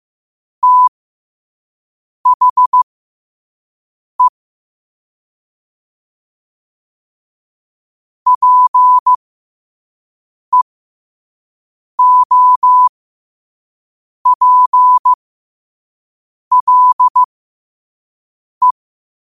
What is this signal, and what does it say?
Morse code, heard as a keyed tone, about 15 wpm, spaced as for 5 wpm, 1000 Hz. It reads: THE PEOPLE
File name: qotdfiles / 5wpm